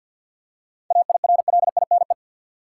Morse code, heard as a keyed tone, about 48 wpm, 700 Hz. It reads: AIRLINE
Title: Seven Letter Words - 50wpm